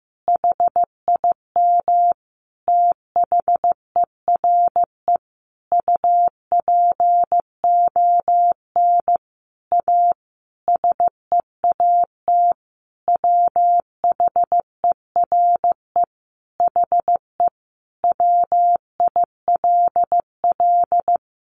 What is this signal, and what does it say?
Morse code, heard as a keyed tone, about 15 wpm, 700 Hz. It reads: HIM THERE UPON A SEAT WHERE HE WILL
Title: Sentences from Top 1500 Words 15wpm